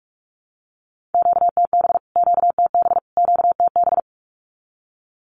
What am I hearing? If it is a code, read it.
Morse code, heard as a keyed tone, about 44 wpm, 700 Hz. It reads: QTB QTB QTB
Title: Q-Codes Repeated 3x 45wpm